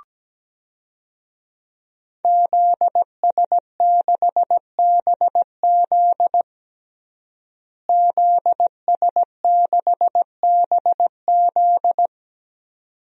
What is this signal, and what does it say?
Morse code, heard as a keyed tone, about 17 wpm, 700 Hz. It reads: ZS6BZ ZS6BZ
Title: DXCC-100 Call Signs - 5 Characters - Repeated 2x - 17wpm